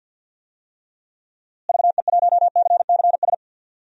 Morse code, heard as a keyed tone, about 49 wpm, 700 Hz. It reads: VE1CCS
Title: DXCC-100 Call Signs - 6 Characters - Repeated 2x - 50wpm